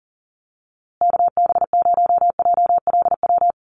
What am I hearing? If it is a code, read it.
Morse code, heard as a keyed tone, about 40 wpm, 700 Hz. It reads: KB0JLW
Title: USA Call Signs Repeated 2x 40wpm